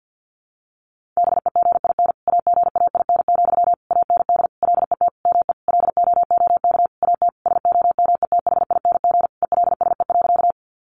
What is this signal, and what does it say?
Morse code, heard as a keyed tone, about 50 wpm, 700 Hz. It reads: BEGIN AGAIN, AND LET ME LOOK AT SOMETHING ELSE.
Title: Sentences from Top 800 Words 50wpm